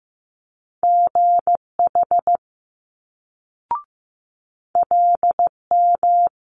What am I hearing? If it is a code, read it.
Morse code, heard as a keyed tone, about 15 wpm, 700 Hz. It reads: GH LM
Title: Multi-letter Phonemes Quick Fire 15wpm